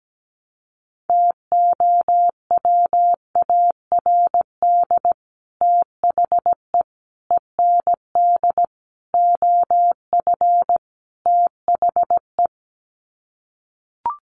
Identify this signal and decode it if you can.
Morse code, heard as a keyed tone, about 17 wpm, 700 Hz. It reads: TOWARD THE END OF THE
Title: Sets of 5 Words 17wpm